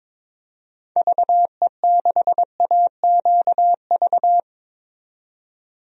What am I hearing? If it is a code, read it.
Morse code, heard as a keyed tone, about 22 wpm, 700 Hz. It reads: VE6AQV
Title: DXCC-100 Call Signs - 6 Characters - 22wpm